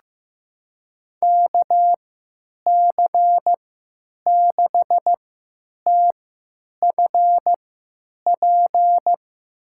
Morse code, heard as a keyed tone, about 15 wpm, spaced as for 9 wpm, 700 Hz. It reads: KC6TFP
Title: USA Call Signs - 3x Character Spacing - 15wpm